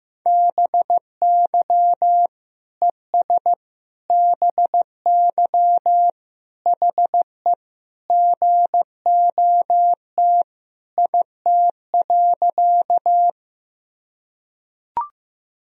Morse code, heard as a keyed tone, about 15 wpm, 700 Hz. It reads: BY ES BY HE GOT IT.
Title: Sentences from Top 300 Words and Abbreviations 15wpm